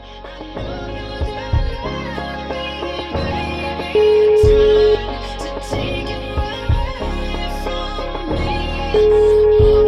Ringback tone (Europe, 425 Hz).